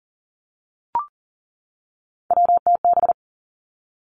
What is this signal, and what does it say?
Morse code, heard as a keyed tone, about 40 wpm, 700 Hz. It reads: WTB
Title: Three Challenging Letter-Number Combinations 40wpm